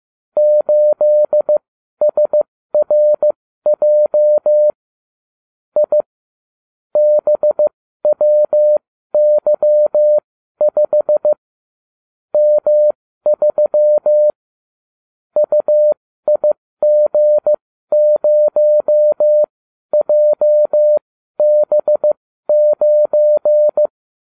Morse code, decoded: 8SRJ I BWY5 M3 UIG0JB9

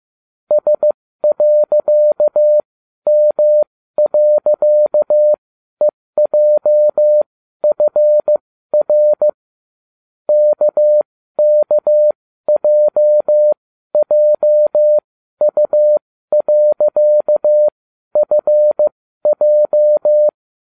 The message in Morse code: S.M.EJFR KKJJU.FJ

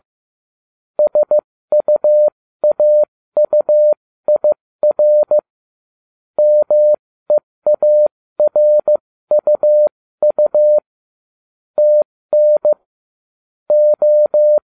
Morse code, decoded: SUAUIR MEARUU TN O

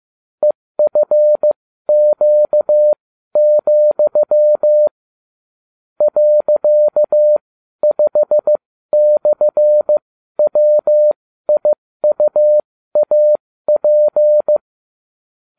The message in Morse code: EFQ, .5/WIUAP

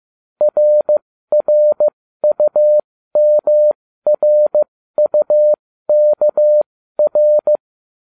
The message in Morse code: RRUMRUKR

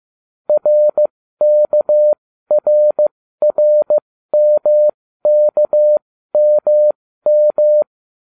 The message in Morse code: RKRRMKMM

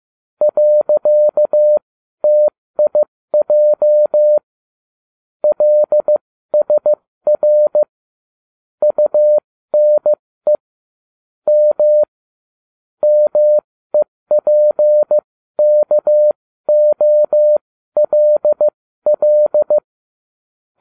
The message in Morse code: .TIJ LSR UNE M MEPKOLL

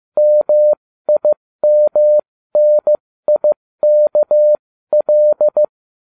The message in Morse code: MIMNIKL

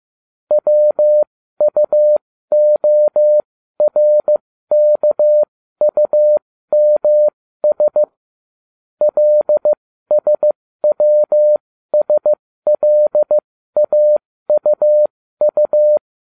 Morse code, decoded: WUORKUMS LSWSLAUU

